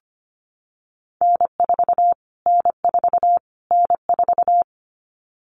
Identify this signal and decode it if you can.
Morse code, heard as a keyed tone, about 25 wpm, 700 Hz. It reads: N4 N4 N4
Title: Callsign Prefixes for Top 25 DXCC - Repeated 3x - 25wpm